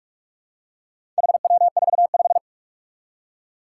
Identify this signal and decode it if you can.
Morse code, heard as a keyed tone, about 44 wpm, 700 Hz. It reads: HW45
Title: Four Challenging Letter-Number Combinations 45wpm